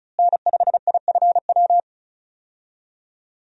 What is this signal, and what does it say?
Morse code, heard as a keyed tone, about 35 wpm, 700 Hz. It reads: N5IFW
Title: USA Call Signs - 5 Characters - Repeated 2x - 35wpm